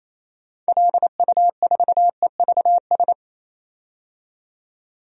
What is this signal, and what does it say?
Morse code, heard as a keyed tone, about 28 wpm, 700 Hz. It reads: LU4EVS
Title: DXCC-100 Call Signs - 6 Characters - 28wpm